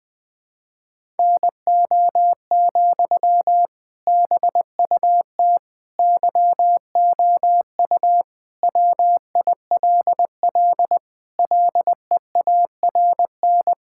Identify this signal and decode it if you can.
Morse code, heard as a keyed tone, about 20 wpm, 700 Hz. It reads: NO, BUT YOU WILL LEARN
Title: Sentences from Top 1000 Words - Review Part 6 - 20wpm